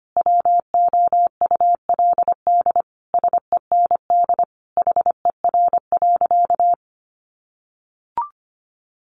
Morse code, heard as a keyed tone, about 25 wpm, 700 Hz. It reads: WOULD SEND HER.
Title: Sentences from Top 1500 Words 25wpm